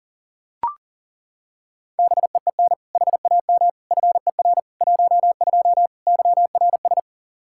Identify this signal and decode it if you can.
Morse code, heard as a keyed tone, about 40 wpm, 700 Hz. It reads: BEEN HAM FER 12 YRS
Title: Practice QSO - Exchange 3 - More Info - 40wpm